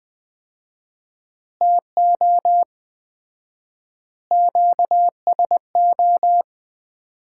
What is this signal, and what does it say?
Morse code, heard as a keyed tone, about 20 wpm, 700 Hz. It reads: TO QSO
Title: Common QSO Phrases - 4x Word-Spacing - 20wpm